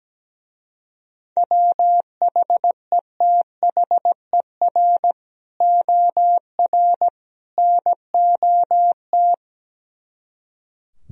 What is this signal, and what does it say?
Morse code, heard as a keyed tone, about 17 wpm, 700 Hz. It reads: WHETHER OR NOT
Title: Speed-Racing - Sets of 3 Words 17wpm